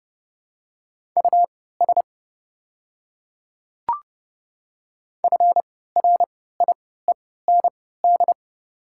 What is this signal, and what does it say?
Morse code, heard as a keyed tone, about 30 wpm, spaced as for 17 wpm, 700 Hz. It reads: US FRIEND
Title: 3x Character Spacing - Top 300 Words - 30wpm